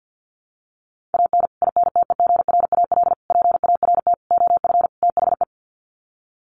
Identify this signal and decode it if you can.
Morse code, heard as a keyed tone, about 50 wpm, 700 Hz. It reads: AN INTEGRAL PART OF THE
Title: Sets of 5 Words 50wpm